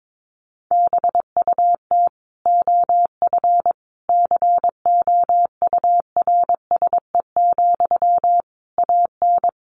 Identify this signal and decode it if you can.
Morse code, heard as a keyed tone, about 22 wpm, 700 Hz. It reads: BUT OF COURSE, AN
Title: Sentences from Top 1500 Words 22wpm